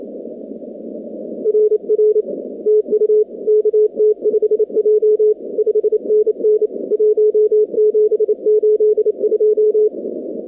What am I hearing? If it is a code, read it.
Morse code, heard as a keyed tone, about 30 wpm, 430 Hz. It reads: RR TU KT5J 5NN 1782
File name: contests / 2019-cw-wpx-kt5j-k5pi- -2330 0-46